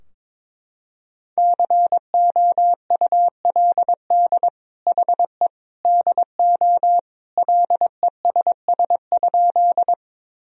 Morse code, heard as a keyed tone, about 22 wpm, 700 Hz. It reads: COULD HE DO LESS?
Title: Sentences from Top 1000 Words - Review Part 2 - 22wpm